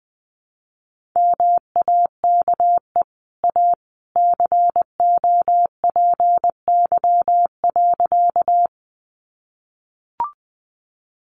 Morse code, heard as a keyed tone, about 20 wpm, 700 Hz. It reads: MAKE A COPY.
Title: Sentences from Top 1400 Words 20wpm